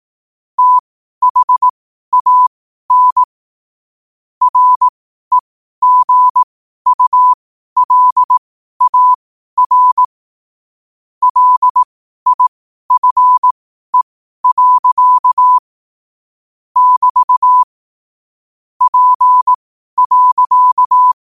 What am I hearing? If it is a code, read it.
Morse code, heard as a keyed tone, about 18 wpm, spaced as for 12 wpm, 1000 Hz. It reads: THAN REGULAR LIFE. <BT> P.
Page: Quotes for Fri, 15 Aug 2025 in Morse Code at 12 words per minute.